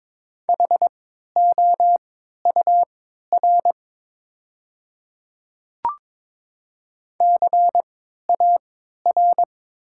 Morse code, heard as a keyed tone, about 23 wpm, spaced as for 13 wpm, 690 Hz. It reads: HOUR CAR
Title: 3x Character Spacing - Top 500 Words - 22wpm